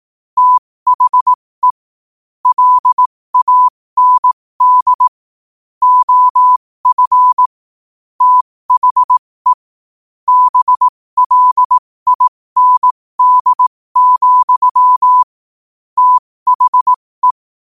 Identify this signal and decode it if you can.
Morse code, heard as a keyed tone, about 18 wpm, spaced as for 15 wpm, 1000 Hz. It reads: THE LAND OF THE BLIND, THE
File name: qotdfiles / 15wpm